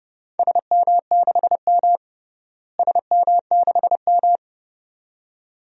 Morse code, heard as a keyed tone, about 31 wpm, 700 Hz. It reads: SM6M SM6M
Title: DXCC-100 Call Signs - 4 Characters - Repeated 2x - 30wpm